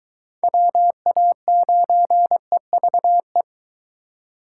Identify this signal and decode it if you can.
Morse code, heard as a keyed tone, about 21 wpm, 700 Hz. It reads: WA9EVE